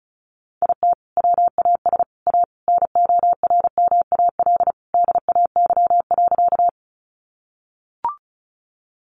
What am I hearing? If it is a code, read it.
Morse code, heard as a keyed tone, about 34 wpm, 700 Hz. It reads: IT WAS A NORMAL DAY.
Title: Sentences from Top 1200 Words 35wpm